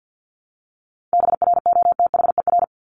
Morse code, heard as a keyed tone, about 50 wpm, 700 Hz. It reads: BROTHER
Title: Top 1751-2000 Words 50wpm